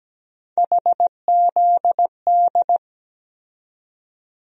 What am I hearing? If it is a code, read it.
Morse code, heard as a keyed tone, about 17 wpm, 700 Hz. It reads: HZD
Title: USA Call Sign Postfixes - 3 Letters 17wpm